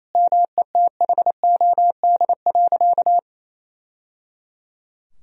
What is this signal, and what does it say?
Morse code, heard as a keyed tone, about 28 wpm, 700 Hz. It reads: METHOD.
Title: Sentences from Top 1000 Words - Review Part 3 - 28wpm